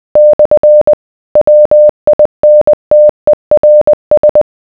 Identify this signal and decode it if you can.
Morse code, heard as a keyed tone, about 20 wpm, 600 Hz. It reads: / WINTERS